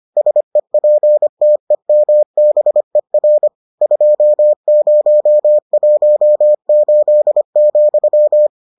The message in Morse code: SEPTEMBER 2018,

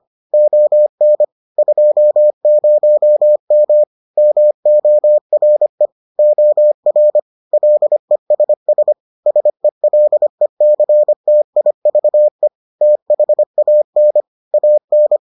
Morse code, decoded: ON 20M MORE OR LESS SELECTIVE THAN AN

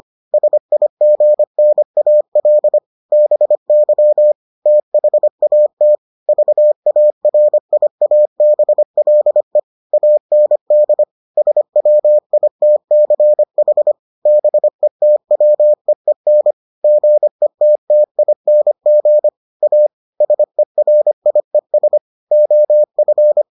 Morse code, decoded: SIGNAL BY THAT VARIABLE AND SWITCH BETWEEN GETTING A SERIES OF

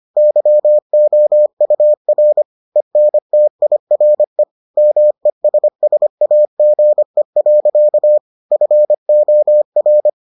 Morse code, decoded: YOUR ENTIRE MESSAGE. FOR

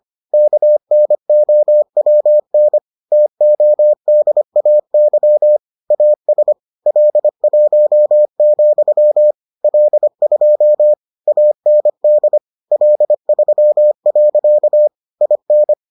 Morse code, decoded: KNOWN TODAY AS L1, L2 AND L3. IN

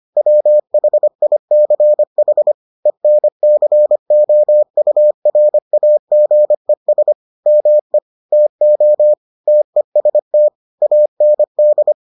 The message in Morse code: WHICH ENCOURAGES ME TO TEST AND

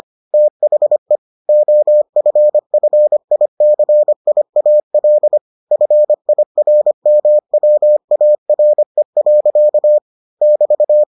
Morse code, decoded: THE OFFICIAL FIRMWARE. <BT>

